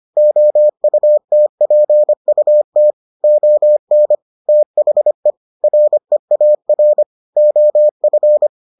Morse code, heard as a keyed tone, about 25 wpm, 600 Hz. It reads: OUTPUT ON THE REAR OF